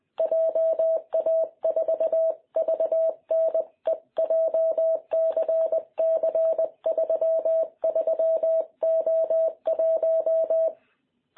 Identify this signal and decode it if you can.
Morse code, heard as a keyed tone, about 20 wpm, 650 Hz. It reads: JA4VNE JCC33O1
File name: jh8cbh / 2021cw